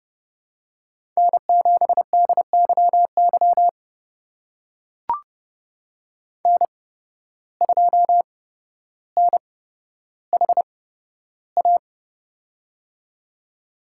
Morse code, heard as a keyed tone, about 30 wpm, spaced as for 13 wpm, 700 Hz. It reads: N7DYY N2NHA
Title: USA Call Signs - 5 Characters - 8x Character Spacing - 30wpm